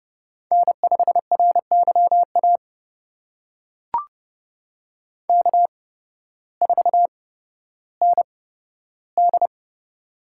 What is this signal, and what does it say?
Morse code, heard as a keyed tone, about 31 wpm, spaced as for 14 wpm, 700 Hz. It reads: N5RYA K4ND
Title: USA Call Signs - 8x Character Spacing - 30wpm